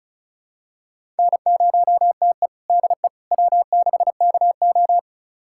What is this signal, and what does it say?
Morse code, heard as a keyed tone, about 36 wpm, 700 Hz. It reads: N0TE DE W6KO
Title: USA Callsign DE Callsign - 4 Characters - 35wpm